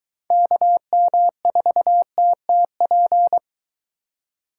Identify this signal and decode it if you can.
Morse code, heard as a keyed tone, about 23 wpm, 700 Hz. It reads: KM4TTP